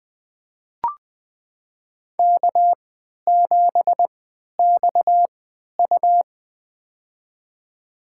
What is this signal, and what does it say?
Morse code, heard as a keyed tone, about 20 wpm, spaced as for 11 wpm, 700 Hz. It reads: K7XU
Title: USA Call Signs - 3x Character Spacing - 20wpm